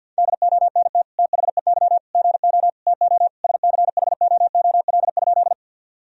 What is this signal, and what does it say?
Morse code, heard as a keyed tone, about 52 wpm, 700 Hz. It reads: DONT THEY GO TO SCHOOL?
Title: Sentences from Top 1000 Words - Review Part 1 - 50wpm